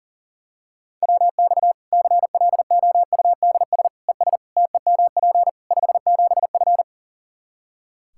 Morse code, heard as a keyed tone, about 40 wpm, 700 Hz. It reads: WX CLOUDS ES TEMP 57F
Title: Practice QSO - Exchange 2 - Info - 40wpm